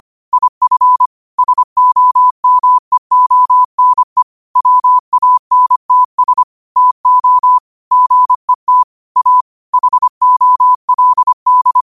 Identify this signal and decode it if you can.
Morse code, heard as a keyed tone, about 25 wpm, 1000 Hz. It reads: IF SOMEONE WANTS TO GET A HOLD